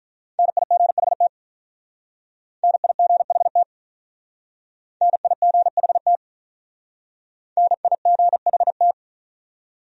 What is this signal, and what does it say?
Morse code, heard as a keyed tone, about 48 wpm, 700 Hz. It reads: NIGHT NIGHT NIGHT NIGHT